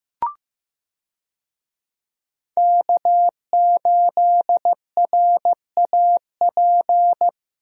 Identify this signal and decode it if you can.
Morse code, heard as a keyed tone, about 16 wpm, 700 Hz. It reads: K8RAP
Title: USA Call Signs - 5 Characters - Repeated 3x - 15wpm